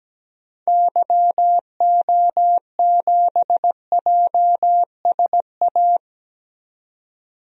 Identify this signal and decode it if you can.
Morse code, heard as a keyed tone, about 17 wpm, 700 Hz. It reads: YO7JSA
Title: DXCC-100 Call Signs - 6 Characters - 17wpm